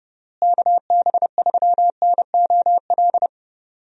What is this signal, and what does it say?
Morse code, heard as a keyed tone, about 28 wpm, 700 Hz. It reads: KB3NOL
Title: USA Call Signs - 6 Characters - Repeated 2x - 30wpm